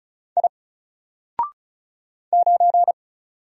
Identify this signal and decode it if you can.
Morse code, heard as a keyed tone, about 34 wpm, 700 Hz. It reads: I 9
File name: Single-Letter-Number-35wpm